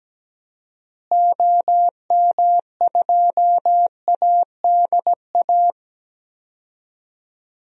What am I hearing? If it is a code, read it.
Morse code, heard as a keyed tone, about 17 wpm, 700 Hz. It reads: OM2ADA
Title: DXCC-100 Call Signs - 6 Characters - Repeated 3x - 17wpm